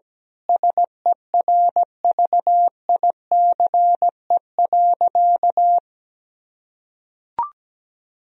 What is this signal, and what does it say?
Morse code, heard as a keyed tone, about 17 wpm, 700 Hz. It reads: SERVICE.
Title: Sentences from Top 1000 Words - Review Part 4 - 17wpm